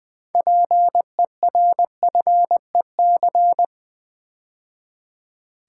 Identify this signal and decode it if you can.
Morse code, heard as a keyed tone, about 20 wpm, 700 Hz. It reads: PERFEC
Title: Build A Word - 3 Letters At A Time - 9-Letter Words - 20wpm